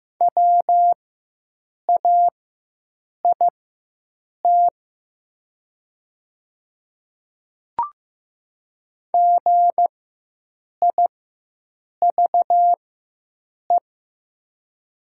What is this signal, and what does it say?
Morse code, heard as a keyed tone, about 15 wpm, spaced as for 7 wpm, 690 Hz. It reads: WAIT GIVE
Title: 4x Character Spacing - Top 500 Words - 15wpm